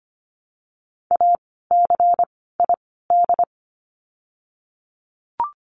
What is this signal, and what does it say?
Morse code, heard as a keyed tone, about 25 wpm, 700 Hz. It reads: A C I D
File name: course / 25:10wpm / 094 Lesson 16f - Letter c - Words 25wpm